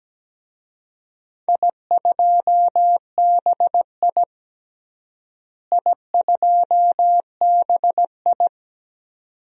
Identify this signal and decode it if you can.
Morse code, heard as a keyed tone, about 17 wpm, 700 Hz. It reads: I2BI I2BI
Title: DXCC-100 Call Signs - 4 Characters - Repeated 2x - 17wpm